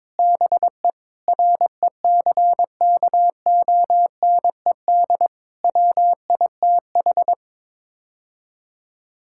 Morse code, heard as a keyed tone, about 22 wpm, 700 Hz. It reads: BE RECKONED WITH
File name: practice / 22wpm / Sets of 6 Words - 22wpm